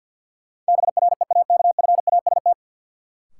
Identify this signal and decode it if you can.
Morse code, heard as a keyed tone, about 53 wpm, 700 Hz. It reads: BREAKFAST
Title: Sentences from Top 1500 Words - New Word Drill - 50wpm